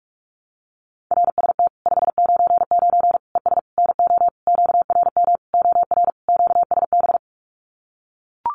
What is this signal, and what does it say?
Morse code, heard as a keyed tone, about 44 wpm, 700 Hz. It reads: RST 599 ES NO QRM OR QSB
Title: Common QSO Phrases 45wpm